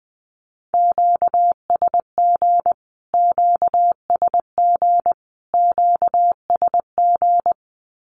Morse code, heard as a keyed tone, about 20 wpm, 700 Hz. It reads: QSG QSG QSG
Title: Q-Codes Repeated 3x 20wpm